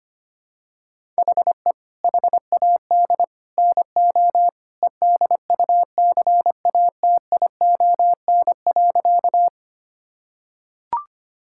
Morse code, heard as a keyed tone, about 25 wpm, 700 Hz. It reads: HE HAD NO EDUCATION.
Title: Sentences from Top 1000 Words - Review Part 6 - 25wpm